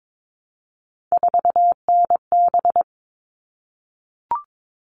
Morse code, heard as a keyed tone, about 22 wpm, 700 Hz. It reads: 4NB
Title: All Cut Numbers - 3-Digit 22wpm